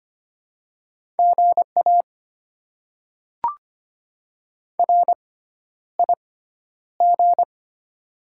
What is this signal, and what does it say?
Morse code, heard as a keyed tone, about 25 wpm, spaced as for 11 wpm, 700 Hz. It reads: GA RIG
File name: Top 20 QSO Elements - 6x Character Spacing - 25wpm